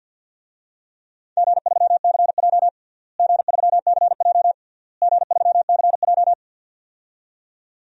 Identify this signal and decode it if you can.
Morse code, heard as a keyed tone, about 50 wpm, 700 Hz. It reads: G3CJ G3CJ G3CJ
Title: DXCC-100 Call Signs - 4 Characters - Repeated 3x - 50wpm